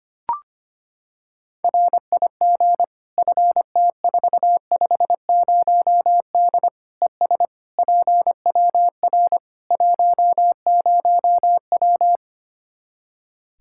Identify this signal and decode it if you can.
Morse code, heard as a keyed tone, about 25 wpm, 700 Hz. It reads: RIG FT450D ES PWR 10W
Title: Practice QSO - Exchange 2 - Info - 25wpm